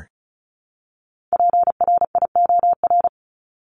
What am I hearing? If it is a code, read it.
Morse code, heard as a keyed tone, about 34 wpm, 700 Hz. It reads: PRIOR
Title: Top 1501-1750 Words 35wpm